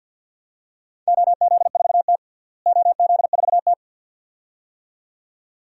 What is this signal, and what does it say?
Morse code, heard as a keyed tone, about 52 wpm, 700 Hz. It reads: OZ4T OZ4T
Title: DXCC-100 Call Signs - 4 Characters - Repeated 2x - 50wpm